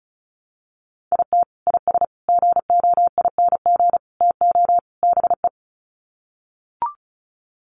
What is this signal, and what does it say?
Morse code, heard as a keyed tone, about 34 wpm, 700 Hz. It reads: IT IS GOING TO BE
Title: Sets of 5 Words 35wpm